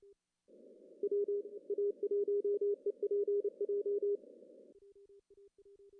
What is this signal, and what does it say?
Morse code, heard as a keyed tone, about 28 wpm, 410 Hz. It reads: WA1EPJ
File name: CW / cqww-cw-k5tr